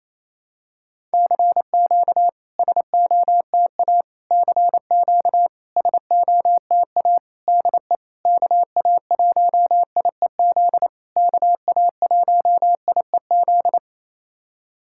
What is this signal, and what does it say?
Morse code, heard as a keyed tone, about 28 wpm, 690 Hz. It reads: CQ SOTA CQ SOTA DE KA1IEZ KA1IEZ
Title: SOTA QSO Practice 28wpm